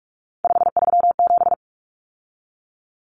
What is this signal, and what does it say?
Morse code, heard as a keyed tone, about 44 wpm, 700 Hz. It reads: 537
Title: Numbers - 3-Digit 45wpm